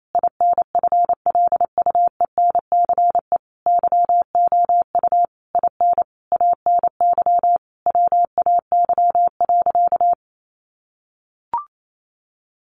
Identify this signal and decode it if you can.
Morse code, heard as a keyed tone, about 28 wpm, 700 Hz. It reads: INFLUENCE YOU IN ANY WAY.
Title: Sentences from Top 1100 Words 28wpm